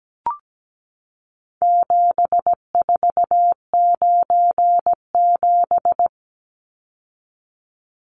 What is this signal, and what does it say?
Morse code, heard as a keyed tone, about 17 wpm, 700 Hz. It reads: 7497
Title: Numbers - 4-Digit 17wpm